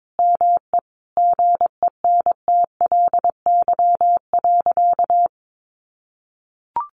Morse code, heard as a keyed tone, about 22 wpm, 700 Hz. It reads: ME GENTLY.
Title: Sentences from Top 1500 Words 22wpm